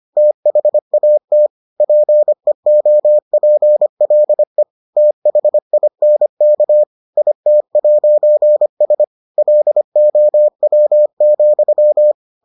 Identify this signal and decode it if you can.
Morse code, heard as a keyed tone, about 25 wpm, 600 Hz. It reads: THAT PEOPLE THINK IT'S LOW,